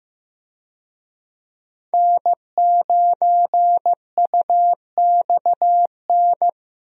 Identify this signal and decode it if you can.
Morse code, heard as a keyed tone, about 16 wpm, 700 Hz. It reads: N9UXN
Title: USA Call Signs - 5 Characters - 15wpm